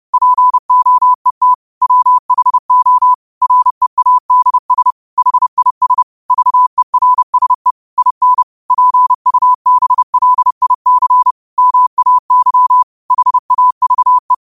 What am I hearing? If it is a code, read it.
Morse code, heard as a keyed tone, about 30 wpm, 1000 Hz. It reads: POET WHO READS HIS VERSE IN PUBLIC MAY HAVE
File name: qotdfiles / 30wpm